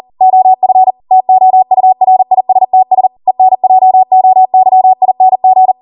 Morse code, heard as a keyed tone, about 40 wpm, 750 Hz. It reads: OF TOURISTS ENJOYING